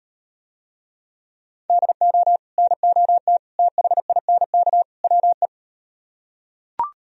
Morse code, heard as a keyed tone, about 37 wpm, 700 Hz. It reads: DO NOT THINK WE